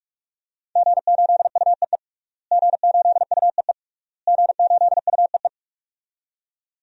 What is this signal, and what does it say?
Morse code, heard as a keyed tone, about 46 wpm, 700 Hz. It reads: G8UEE G8UEE G8UEE
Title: DXCC-100 Call Signs - 5 Characters - Repeated 3x - 45wpm